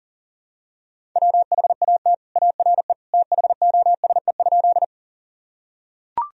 Morse code, heard as a keyed tone, about 37 wpm, 700 Hz. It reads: WHAT ARE THOSE?
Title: Sentences from Top 1000 Words - Review Part 1 - 40wpm